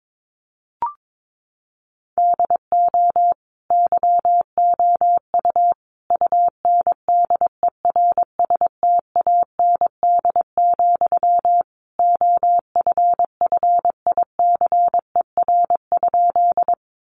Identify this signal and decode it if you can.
Morse code, heard as a keyed tone, about 22 wpm, 700 Hz. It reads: DO YOU UNDERSTAND, OFFICER?
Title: Sentences from Top 1200 Words 22wpm